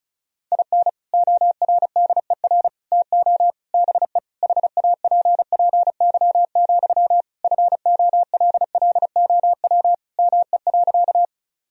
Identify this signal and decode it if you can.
Morse code, heard as a keyed tone, about 36 wpm, 700 Hz. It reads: IN ORDER TO BE HAPPY, FOLLOW ME.
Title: Sentences from Top 1000 Words - Review Part 2 - 35wpm